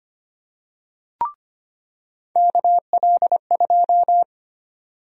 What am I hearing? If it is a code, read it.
Morse code, heard as a keyed tone, about 25 wpm, 700 Hz. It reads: KL2
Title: USA Call Sign Prefixes - 25wpm